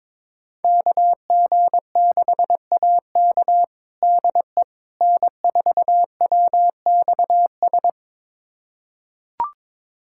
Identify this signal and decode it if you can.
Morse code, heard as a keyed tone, about 22 wpm, 700 Hz. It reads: KG6AK DE N4WXS
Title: USA Callsign DE Callsign - 5 Characters - 22wpm